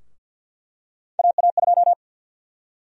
Morse code, heard as a keyed tone, about 50 wpm, 700 Hz. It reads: AA2
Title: USA Call Sign Prefixes - 50wpm